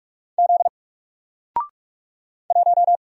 Morse code, heard as a keyed tone, about 45 wpm, 700 Hz. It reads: Z 1
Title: Single Letter-Number Alt 45wpm